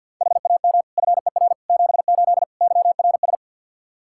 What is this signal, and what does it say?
Morse code, heard as a keyed tone, about 52 wpm, 700 Hz. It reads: HAM FER 78 YRS
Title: Practice QSO - Exchange 3 - More Info - 50wpm